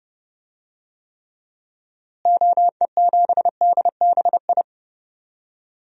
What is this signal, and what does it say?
Morse code, heard as a keyed tone, about 32 wpm, 700 Hz. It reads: OE7DBI
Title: DXCC-100 Call Signs - 6 Characters - Repeated 3x - 30wpm